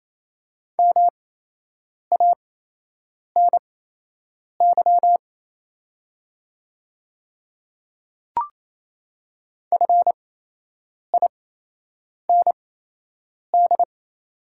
Morse code, heard as a keyed tone, about 27 wpm, spaced as for 8 wpm, 700 Hz. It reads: MANY FIND
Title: 8x Character Spacing - Top 100 Words - 28wpm